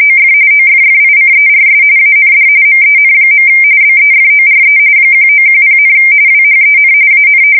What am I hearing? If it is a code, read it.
RTTY, about 45.45 baud. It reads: OF OFFICERS AND MEN. BEFORE 1910, WHEN AN
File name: fsk170 / usnavy3